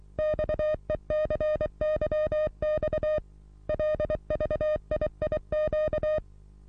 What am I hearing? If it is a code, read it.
Morse code, heard as a keyed tone, about 23 wpm, 610 Hz. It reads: XECYX LVIIQ
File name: repos / Morsekurs / DJ4UF Morsen27